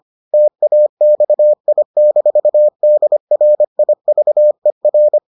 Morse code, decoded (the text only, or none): TAXI-DRIVER